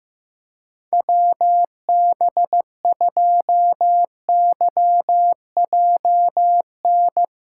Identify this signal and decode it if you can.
Morse code, heard as a keyed tone, about 16 wpm, 700 Hz. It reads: WB2YJN
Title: USA Callsign DE Callsign - 6 Characters - 15wpm